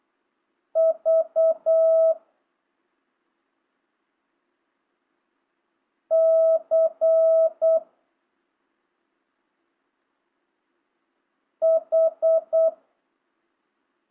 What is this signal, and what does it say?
Morse code, heard as a keyed tone, about 8 wpm, 650 Hz. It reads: V C H